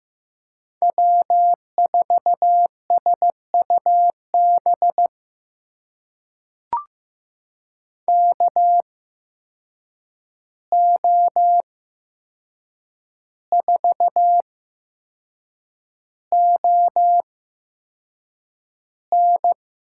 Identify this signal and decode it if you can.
Morse code, heard as a keyed tone, about 15 wpm, spaced as for 6 wpm, 700 Hz. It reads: W4SUB KO4ON
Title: USA Call Signs - 5 Characters - 8x Character Spacing - 15wpm